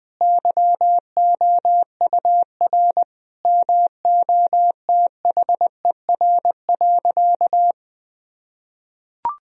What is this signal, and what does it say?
Morse code, heard as a keyed tone, about 20 wpm, 700 Hz. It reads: YOUR MOTHER.
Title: Sentences from Top 1000 Words - Review Part 4 - 20wpm